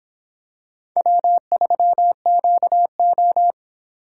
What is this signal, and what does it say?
Morse code, heard as a keyed tone, about 26 wpm, 700 Hz. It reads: W3QO